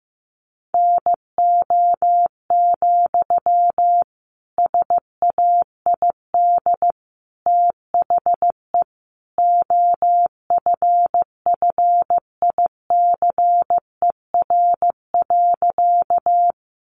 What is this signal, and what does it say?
Morse code, heard as a keyed tone, about 15 wpm, 700 Hz. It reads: NO, SAID THE OFFICER.
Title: Sentences from Top 1200 Words 15wpm